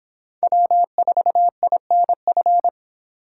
Morse code, heard as a keyed tone, about 25 wpm, 700 Hz. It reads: W4INF